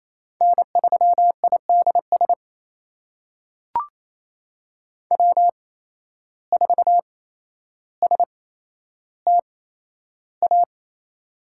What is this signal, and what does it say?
Morse code, heard as a keyed tone, about 28 wpm, spaced as for 12 wpm, 700 Hz. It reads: N3IDS W4STA
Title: USA Call Signs - 5 Characters - 8x Character Spacing - 28wpm